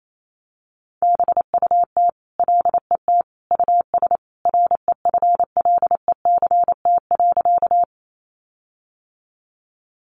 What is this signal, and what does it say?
Morse code, heard as a keyed tone, about 28 wpm, 700 Hz. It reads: BUT LET US REFLECT.
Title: Sentences from Top 1500 Words 28wpm